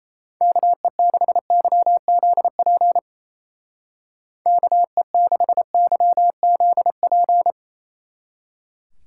Morse code, heard as a keyed tone, about 34 wpm, 700 Hz. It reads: KE6YZP KE6YZP
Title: Speed Racing - USA 2x3 Call Signs 28wpm